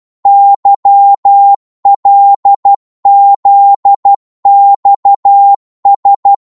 Morse code, heard as a keyed tone, about 12 wpm, 800 Hz. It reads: YLZXS
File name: Morse / ltr-12wpm